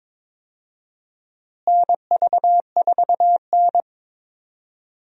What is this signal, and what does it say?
Morse code, heard as a keyed tone, about 22 wpm, 700 Hz. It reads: NV4N
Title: USA Call Signs - 4 Characters - Repeated 2x - 22wpm